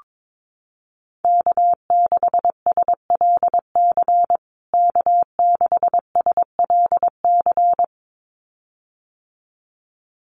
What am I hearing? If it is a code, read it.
Morse code, heard as a keyed tone, about 22 wpm, 700 Hz. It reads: K6SLC K6SLC
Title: USA Call Signs Repeated 2x 22wpm